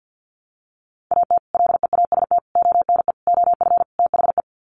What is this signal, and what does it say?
Morse code, heard as a keyed tone, about 50 wpm, 700 Hz. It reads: AT LEAST ONE OF THE
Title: Sets of 5 Words Encore 50wpm